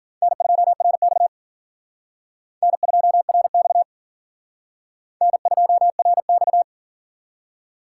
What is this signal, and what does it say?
Morse code, heard as a keyed tone, about 49 wpm, 700 Hz. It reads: N2RX N2RX N2RX
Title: Speed Racing - USA 1x2 Call Signs 40wpm